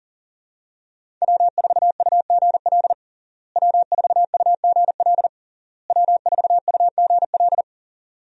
Morse code, heard as a keyed tone, about 37 wpm, 700 Hz. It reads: W4UGL W4UGL W4UGL
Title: USA Call Signs - 5 Characters - Repeated 3x - 40wpm